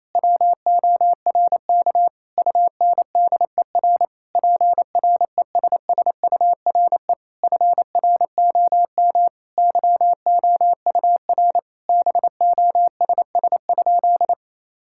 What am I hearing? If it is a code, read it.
Morse code, heard as a keyed tone, about 28 wpm, 700 Hz. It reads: WORK UNDER PRESSURE FROM YOUR BOSS?
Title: Sentences from Top 1000 Words - Review Part 6 - 28wpm